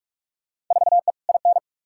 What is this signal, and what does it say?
Morse code, heard as a keyed tone, about 46 wpm, 700 Hz. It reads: 4E IN
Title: Field Day Exchanges - 45wpm